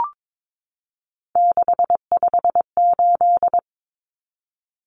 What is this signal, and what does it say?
Morse code, heard as a keyed tone, about 22 wpm, 700 Hz. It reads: 658
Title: Numbers - 3-Digit 22wpm